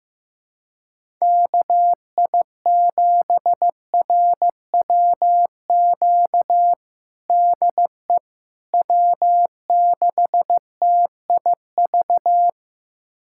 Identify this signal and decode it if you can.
Morse code, heard as a keyed tone, about 15 wpm, 700 Hz. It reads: KI7RWQ DE W6TIV
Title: USA Callsign DE Callsign 15wpm